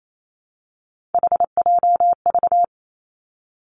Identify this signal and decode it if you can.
Morse code, heard as a keyed tone, about 28 wpm, 700 Hz. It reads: HJV
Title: Three Challenging Letter-Number Combinations 28wpm